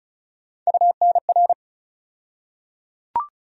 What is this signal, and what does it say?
Morse code, heard as a keyed tone, about 36 wpm, 700 Hz. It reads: UNR
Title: USA Call Sign Postfixes - 3 Letters 35wpm